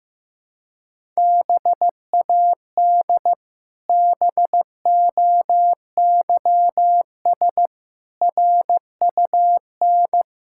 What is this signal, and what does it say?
Morse code, heard as a keyed tone, about 15 wpm, 700 Hz. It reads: BAD BOYS RUN
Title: Pangram Challenge v2 - 15wpm